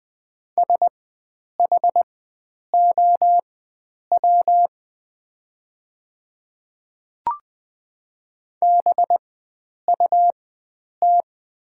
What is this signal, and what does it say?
Morse code, heard as a keyed tone, about 20 wpm, spaced as for 9 wpm, 700 Hz. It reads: SHOW BUT
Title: 4x Character Spacing - Top 300 Words - 20wpm